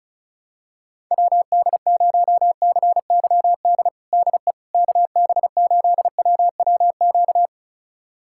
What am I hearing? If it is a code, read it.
Morse code, heard as a keyed tone, about 34 wpm, 700 Hz. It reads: WD0CYD DE KB8WWQ
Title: USA Callsign DE Callsign - 6 Characters - 35wpm